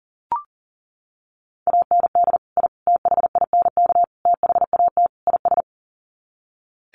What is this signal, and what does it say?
Morse code, heard as a keyed tone, about 40 wpm, 700 Hz. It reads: AND I THINK THAT IS
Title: Sets of 5 Words Encore2 40wpm